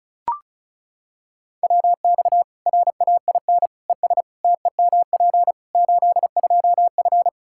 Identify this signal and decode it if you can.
Morse code, heard as a keyed tone, about 34 wpm, 700 Hz. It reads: WX RAIN ES TEMP 82F
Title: Practice QSO - Exchange 2 - Info - 35wpm